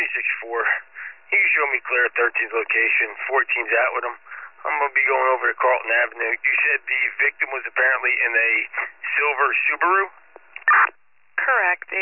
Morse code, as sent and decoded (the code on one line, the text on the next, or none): none